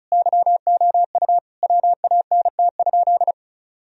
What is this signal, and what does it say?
Morse code, heard as a keyed tone, about 36 wpm, 700 Hz. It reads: YOU WANT?
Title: Sentences from Top 1000 Words - Review Part 4 - 35wpm